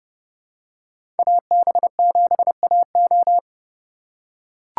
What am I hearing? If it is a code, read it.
Morse code, heard as a keyed tone, about 30 wpm, 700 Hz. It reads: AB7AO
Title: USA Call Signs - 5 Characters - 2x Character Spacing - 30wpm